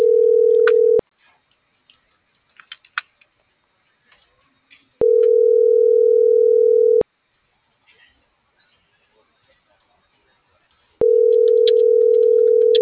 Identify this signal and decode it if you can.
Ringback tone (North America), 440 + 480 Hz.